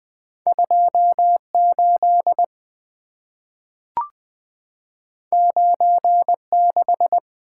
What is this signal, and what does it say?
Morse code, heard as a keyed tone, about 20 wpm, 700 Hz. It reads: 28 96
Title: Two Letter-Number Combinations - 20wpm